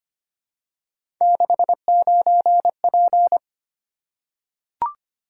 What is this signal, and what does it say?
Morse code, heard as a keyed tone, about 25 wpm, 700 Hz. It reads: 69P
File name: Three Letter-Number Combinations - 25wpm